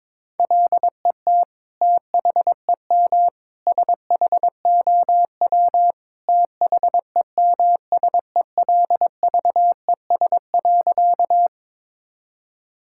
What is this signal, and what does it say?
Morse code, heard as a keyed tone, about 22 wpm, 700 Hz. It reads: LET THEM SHOW THEMSELVES.
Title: Sentences from Top 1000 Words - Review Part 5 - 22wpm